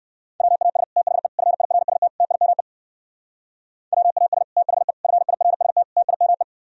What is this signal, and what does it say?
Morse code, heard as a keyed tone, about 58 wpm, 700 Hz. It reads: WAS THE FIRST TIME WAS THE FIRST TIME